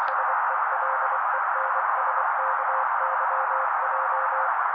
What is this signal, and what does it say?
Morse code, heard as a keyed tone, about 24 wpm, 530 Hz. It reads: SELENSKYJ
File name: daily / 23wpm noise